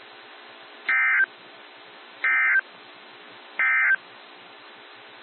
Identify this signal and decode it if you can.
Emergency Alert System (SAME) tones, heard as harsh buzzing bursts. It is heard from 0.8 to 4.0 s.
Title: NOAA Weather Radio - WXJ73